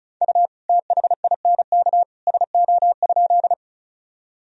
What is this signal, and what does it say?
Morse code, heard as a keyed tone, about 34 wpm, 690 Hz. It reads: U THINK SO?
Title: Sentences from Top 200 Words and Abbreviations 35wpm